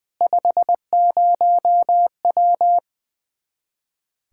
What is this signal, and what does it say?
Morse code, heard as a keyed tone, about 20 wpm, 700 Hz. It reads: 50W
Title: Speed-Racing Common QSO Phrases - 20wpm